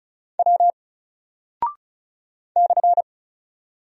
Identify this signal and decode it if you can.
Morse code, heard as a keyed tone, about 36 wpm, 700 Hz. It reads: W /
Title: Single Character (Letter:Number:Punctuation:Prosign) 35wpm